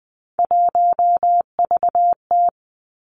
Morse code, heard as a keyed tone, about 20 wpm, 700 Hz. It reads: 1VT
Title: Three Challenging Letter-Number Combinations 20wpm